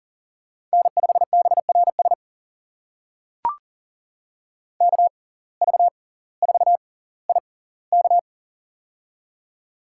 Morse code, heard as a keyed tone, about 40 wpm, spaced as for 20 wpm, 700 Hz. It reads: N5BRS KV4IK
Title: USA Call Signs - 5 Characters - 6x Character Spacing - 40wpm